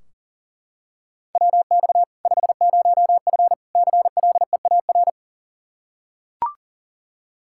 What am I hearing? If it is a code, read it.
Morse code, heard as a keyed tone, about 37 wpm, 700 Hz. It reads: WX 50F CLEAR
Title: K1USN SST QSO Practice 40wpm